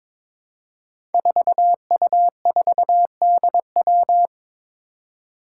Morse code, heard as a keyed tone, about 21 wpm, 700 Hz. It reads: 4U4DW
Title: Five Challenging Letter-Number Combinations 22wpm